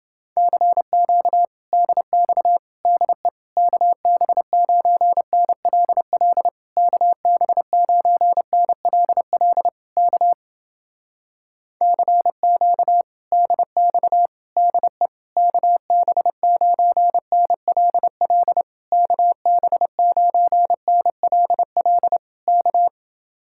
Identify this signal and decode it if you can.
Morse code, heard as a keyed tone, about 29 wpm, 700 Hz. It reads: CQ DX DE KB9NLL KB9NLL K CQ DX DE KB9NLL KB9NLL K